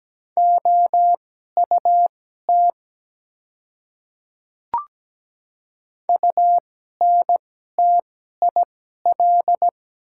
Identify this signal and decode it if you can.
Morse code, heard as a keyed tone, about 17 wpm, spaced as for 12 wpm, 700 Hz. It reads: OUT UNTIL
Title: 2x Character Spacing - Top 300 Words - 17wpm